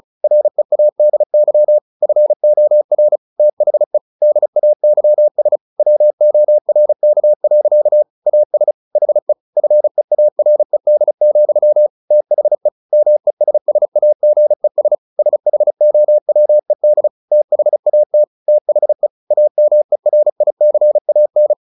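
Morse code, heard as a keyed tone, about 34 wpm, 600 Hz. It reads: READY FOR THE DAYS WORK. AS HE FEARED, THE MESSAGES SHOWED THAT THE AMERICAN
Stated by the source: Morse practice at 35 wpm from CW med Gnister, sending Ken Follett's novel Never in English.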